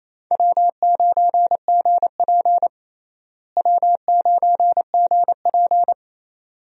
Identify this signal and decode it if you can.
Morse code, heard as a keyed tone, about 28 wpm, 700 Hz. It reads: W9GP W9GP
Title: USA Call Signs - 4 Characters - Repeated 2x - 28wpm